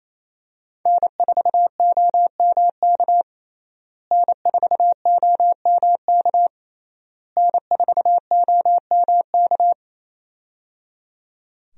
Morse code, heard as a keyed tone, about 28 wpm, 700 Hz. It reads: N4OMK N4OMK N4OMK
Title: USA Call Signs - 5 Characters - Repeated 3x - 28wpm